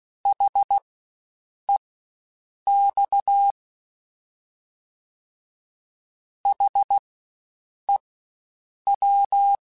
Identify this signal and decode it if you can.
Morse code, heard as a keyed tone, about 16 wpm, spaced as for 7 wpm, 800 Hz. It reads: HEX HEW
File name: Spaced Alphas / Spaced Alphas mp3s / Hs sp @15